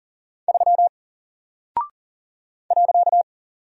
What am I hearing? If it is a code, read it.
Morse code, heard as a keyed tone, about 39 wpm, 700 Hz. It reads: 3 .
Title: Single Character (Letter:Number:Punctuation:Prosign) Alt 40wpm